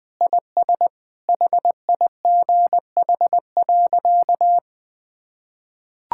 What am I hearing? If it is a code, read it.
Morse code, heard as a keyed tone, about 19 wpm, 700 Hz. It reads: IS HIGH.
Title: Sentences with 3-Word Phrases - 20wpm